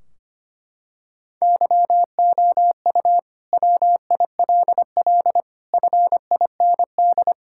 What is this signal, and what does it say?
Morse code, heard as a keyed tone, about 25 wpm, 700 Hz. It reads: YOU WILL FIND
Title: Sentences from Top 1600 Words 25wpm